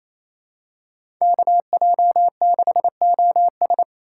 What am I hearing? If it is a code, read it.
Morse code, heard as a keyed tone, about 28 wpm, 700 Hz. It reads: KJ6OS
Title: USA Call Signs - 4x Character Spacing - 28wpm